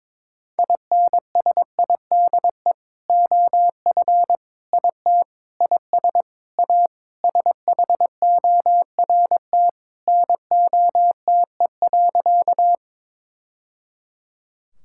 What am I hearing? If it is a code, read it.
Morse code, heard as a keyed tone, about 22 wpm, 700 Hz. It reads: INSIDE OF IT IS A SHORT NOTE.
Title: Sentences from Top 1000 Words - Review Part 6 - 22wpm